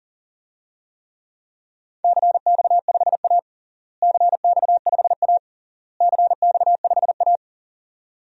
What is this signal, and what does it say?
Morse code, heard as a keyed tone, about 37 wpm, 700 Hz. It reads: CX5A CX5A CX5A
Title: DXCC-100 Call Signs - 4 Characters - Repeated 3x - 40wpm